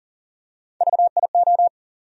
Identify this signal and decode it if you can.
Morse code, heard as a keyed tone, about 40 wpm, 700 Hz. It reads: VIO
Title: USA Call Sign Postfixes - 3 Letters 40wpm